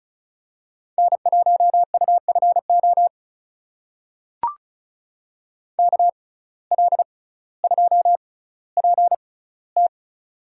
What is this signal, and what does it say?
Morse code, heard as a keyed tone, about 34 wpm, spaced as for 18 wpm, 700 Hz. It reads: N1UFO KL2PT
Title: USA Call Signs - 5 Characters - 6x Character Spacing - 35wpm